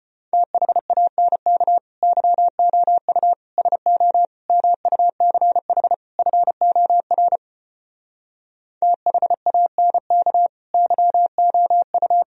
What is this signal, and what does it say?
Morse code, heard as a keyed tone, about 33 wpm, 700 Hz. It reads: THANK YOU SO MUCH FOR THANK YOU